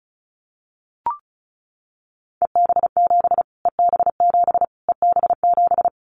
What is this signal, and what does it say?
Morse code, heard as a keyed tone, about 34 wpm, 700 Hz. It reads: EB7 EB7 EB7
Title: Callsign Prefixes for Top 25 DXCC - Repeated 3x - 35wpm